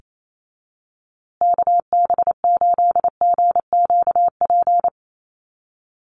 Morse code, heard as a keyed tone, about 28 wpm, 700 Hz. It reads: KB8GQP
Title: USA Call Signs Repeated 3x 28wpm